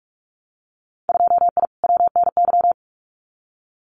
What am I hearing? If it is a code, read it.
Morse code, heard as a keyed tone, about 46 wpm, 700 Hz. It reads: 2I WNY
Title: Winter Field Day Exchanges Encore 45wpm